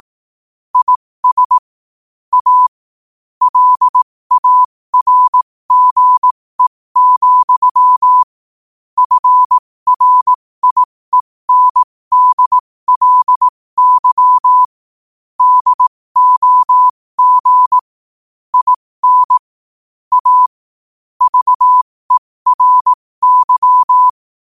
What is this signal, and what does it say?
Morse code, heard as a keyed tone, about 18 wpm, spaced as for 15 wpm, 1000 Hz. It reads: IS A LARGE, FRIENDLY DOG IN A VERY
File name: qotdfiles / 15wpm